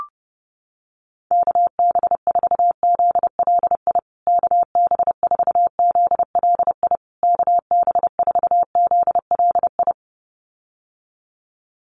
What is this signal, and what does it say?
Morse code, heard as a keyed tone, about 30 wpm, 700 Hz. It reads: KB4ZLI KB4ZLI KB4ZLI
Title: USA Call Signs Repeated 3x 30wpm